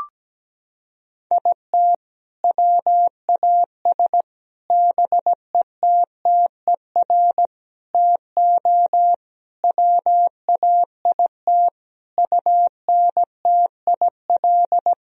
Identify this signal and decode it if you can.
Morse code, heard as a keyed tone, about 17 wpm, 700 Hz. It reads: IT WAS BETTER TO WAIT UNTIL
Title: Sentences from Top 1000 Words - Review Part 1 - 17wpm